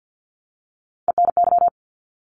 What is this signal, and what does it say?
Morse code, heard as a keyed tone, about 50 wpm, 700 Hz. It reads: ENY
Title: ARRL Sections 50wpm